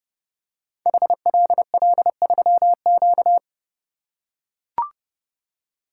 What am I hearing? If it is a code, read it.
Morse code, heard as a keyed tone, about 28 wpm, 700 Hz. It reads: HLL3Q
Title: Code Groups - 5 Letter:Numbers - 30wpm